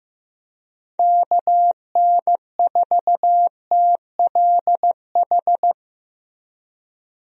Morse code, heard as a keyed tone, about 15 wpm, 700 Hz. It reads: KN4TLH
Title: USA Call Signs - 6 Characters - Repeated 3x - 15wpm